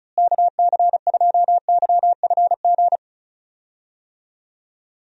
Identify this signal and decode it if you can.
Morse code, heard as a keyed tone, about 36 wpm, 700 Hz. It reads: KC2YFG
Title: USA Call Signs - 6 Characters - Repeated 2x - 35wpm